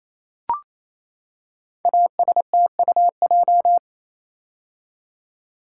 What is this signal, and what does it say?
Morse code, heard as a keyed tone, about 29 wpm, 700 Hz. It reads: ASTUJ
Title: Code Groups - 5 Letters - 28wpm